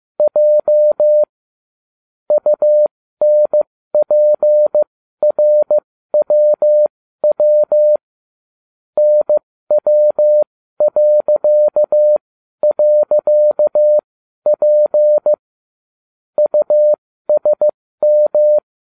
Morse code, heard as a keyed tone, about 15 wpm, spaced as for 12 wpm, 600 Hz. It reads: J UNPRWW NW..P USM